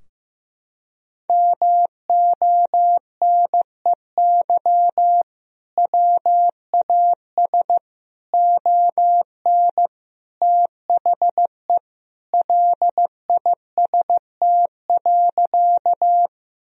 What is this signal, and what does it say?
Morse code, heard as a keyed tone, about 15 wpm, 700 Hz. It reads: MONEY WAS ON THE LIST.
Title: Sentences from Top 1000 Words - Review Part 2 - 15wpm